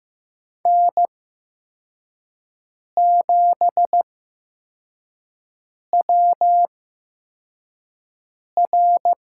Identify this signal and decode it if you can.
Morse code, heard as a keyed tone, about 15 wpm, spaced as for 4 wpm, 700 Hz. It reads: N7WR
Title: USA Call Signs - 5 Characters - 8x Character Spacing - 15wpm